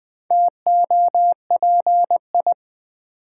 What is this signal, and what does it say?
Morse code, heard as a keyed tone, about 20 wpm, 700 Hz. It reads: TOPI
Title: Build A Word - 1 Letter At A Time - 5-Letter Words - 20wpm